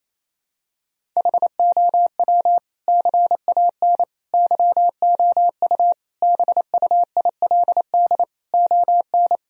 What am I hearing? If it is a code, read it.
Morse code, heard as a keyed tone, about 27 wpm, 700 Hz. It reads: HOW CAN YOU BUILD ON
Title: Sentences from Top 1000 Words - Review Part 1 - 28wpm